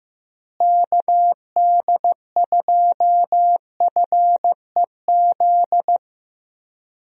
Morse code, heard as a keyed tone, about 15 wpm, 700 Hz. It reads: KD2FEZ
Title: USA Call Signs - 6 Characters - Repeated 2x - 15wpm